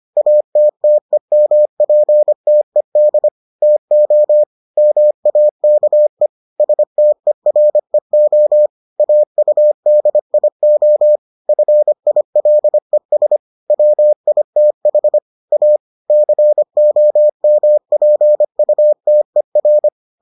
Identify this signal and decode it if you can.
Morse code, heard as a keyed tone, about 25 wpm, 600 Hz. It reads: ATTEMPTED TO MAKE STEREO AUDIO FILES WITH A COMPUTER